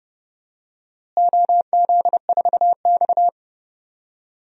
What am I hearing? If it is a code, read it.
Morse code, heard as a keyed tone, about 30 wpm, 700 Hz. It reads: OZ4X
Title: DXCC-100 Call Signs - 4 Characters - Repeated 3x - 30wpm